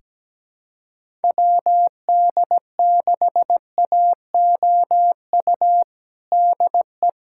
Morse code, heard as a keyed tone, about 17 wpm, 700 Hz. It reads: WD6AOU DE
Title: USA Callsign DE Callsign - 6 Characters - 17wpm